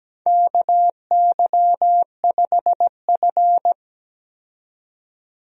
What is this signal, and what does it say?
Morse code, heard as a keyed tone, about 17 wpm, 700 Hz. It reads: KY5F
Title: USA Call Signs - 4 Characters - 17wpm